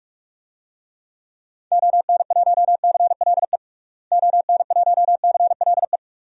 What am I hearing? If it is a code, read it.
Morse code, heard as a keyed tone, about 46 wpm, 700 Hz. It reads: ON1CLE ON1CLE
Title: DXCC-100 Call Signs - 6 Characters - Repeated 2x - 45wpm